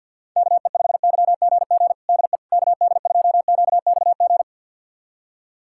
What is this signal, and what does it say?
Morse code, heard as a keyed tone, about 52 wpm, 700 Hz. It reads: KE5YGG DE KD2QXG
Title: USA Callsign DE Callsign - 6 Characters - 50wpm